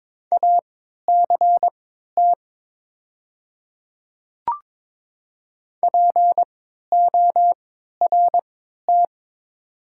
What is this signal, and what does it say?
Morse code, heard as a keyed tone, about 23 wpm, spaced as for 13 wpm, 700 Hz. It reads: ACT PORT
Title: 3x Character Spacing - Top 200 Words - 22wpm